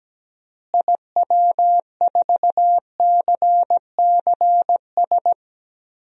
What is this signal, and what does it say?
Morse code, heard as a keyed tone, about 17 wpm, 700 Hz. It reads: IW4CCS
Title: DXCC-100 Call Signs - 6 Characters - Repeated 3x - 17wpm